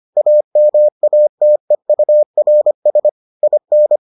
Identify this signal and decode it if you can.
Morse code, heard as a keyed tone, about 25 wpm, 600 Hz. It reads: AMATEURS IN